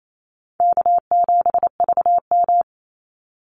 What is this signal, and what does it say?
Morse code, heard as a keyed tone, about 28 wpm, 700 Hz. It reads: K7VM
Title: Easy Call Signs Repeated 2x - 28wpm